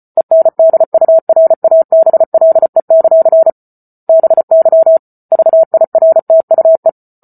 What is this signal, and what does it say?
Morse code, heard as a keyed tone, about 34 wpm, 650 Hz. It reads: ENDURABLE; BY VIRTUE